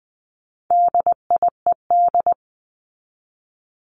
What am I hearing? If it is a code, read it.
Morse code, heard as a keyed tone, about 20 wpm, 700 Hz. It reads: DIED
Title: ED Suffix - Top 100 Words with Base Words 20wpm